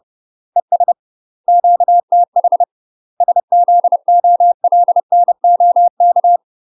Morse code, heard as a keyed tone, about 30 wpm, 700 Hz. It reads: ES QTH SZOLNOK
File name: UK MASTER QSOs TRAINING / 30WPM / UK MASTER QSOs 0048